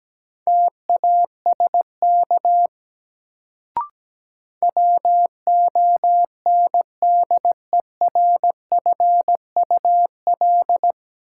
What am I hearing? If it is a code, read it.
Morse code, heard as a keyed tone, about 17 wpm, 700 Hz. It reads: TASK WONDERFUL
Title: Sentences from Top 1100 Words - New Word Drill Alt - 17wpm